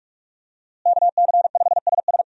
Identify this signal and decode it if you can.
Morse code, heard as a keyed tone, about 43 wpm, 700 Hz. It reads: KC5SS
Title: USA Call Signs - 5 Characters - Repeated 2x - 45wpm